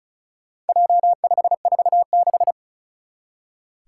Morse code, heard as a keyed tone, about 36 wpm, 700 Hz. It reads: J546
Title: Four Challenging Letter-Number Combinations 35wpm